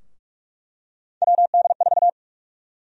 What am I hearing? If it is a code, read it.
Morse code, heard as a keyed tone, about 46 wpm, 700 Hz. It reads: WD4
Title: USA Call Sign Prefixes - Repeated 2x - 45wpm